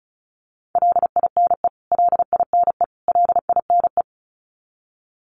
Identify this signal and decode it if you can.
Morse code, heard as a keyed tone, about 34 wpm, 700 Hz. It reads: LINE LINE LINE
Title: Top 500 Words Repeated 35wpm